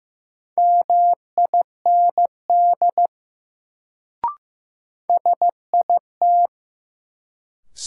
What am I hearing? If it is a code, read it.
Morse code, heard as a keyed tone, about 15 wpm, 700 Hz. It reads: MIND SIT
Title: Sentences from Top 500 Words - New Word Drill - 15wpm